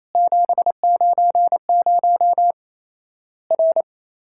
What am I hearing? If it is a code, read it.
Morse code, heard as a keyed tone, about 28 wpm, 700 Hz. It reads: 790 R
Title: Worked All Europe DX Contest - QTC Exchange - 28wpm